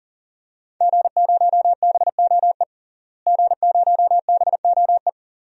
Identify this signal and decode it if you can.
Morse code, heard as a keyed tone, about 40 wpm, 700 Hz. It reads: G0BOE G0BOE
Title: DXCC-100 Call Signs - 5 Characters - Repeated 2x - 40wpm